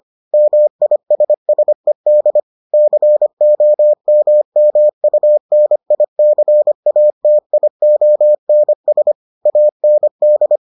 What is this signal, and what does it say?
Morse code, heard as a keyed tone, about 25 wpm, 600 Hz. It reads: MISSED COMMUNICATIONS AND